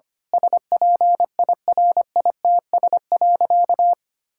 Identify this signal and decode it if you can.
Morse code, heard as a keyed tone, about 25 wpm, 700 Hz. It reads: SPIRITS.
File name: practice / 25wpm / Difficult Sentences with Hints - 25wpm